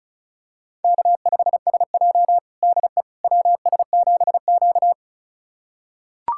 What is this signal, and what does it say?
Morse code, heard as a keyed tone, about 36 wpm, 700 Hz. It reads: K5SJ DE WS7Q
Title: USA Callsign DE Callsign - 4 Characters - 35wpm